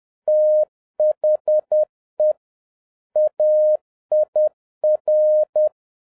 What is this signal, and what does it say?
Morse code, heard as a keyed tone, about 10 wpm, 610 Hz. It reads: THE AIR
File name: 2011-07-mp3 / n41 EXCITING QRP CW OPERATION 10wpm